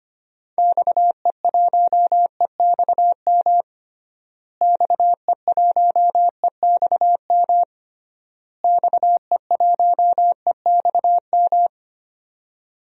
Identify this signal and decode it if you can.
Morse code, heard as a keyed tone, about 25 wpm, 700 Hz. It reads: XE1EXM XE1EXM XE1EXM
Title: DXCC-100 Call Signs - 6 Characters - Repeated 3x - 25wpm